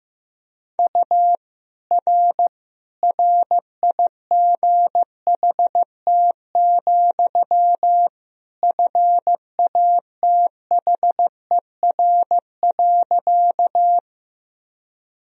Morse code, heard as a keyed tone, about 15 wpm, 700 Hz. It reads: U R RIGHT, FATHER.
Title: Sentences from Top 300 Words and Abbreviations 15wpm